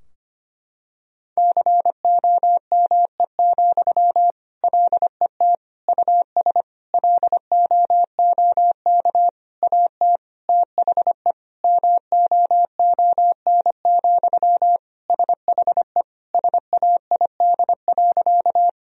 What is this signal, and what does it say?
Morse code, heard as a keyed tone, about 25 wpm, 700 Hz. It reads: COME, LET US LOOK AT THE MOON, SHE SAID.
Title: Sentences from Top 1000 Words - Review Part 3 - 25wpm